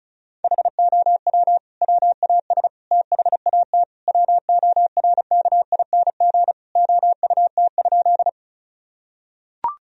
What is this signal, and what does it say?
Morse code, heard as a keyed tone, about 34 wpm, 700 Hz. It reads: HOW WAS THAT WORKING OUT?
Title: Sentences from Top 1000 Words - Review Part 4 - 35wpm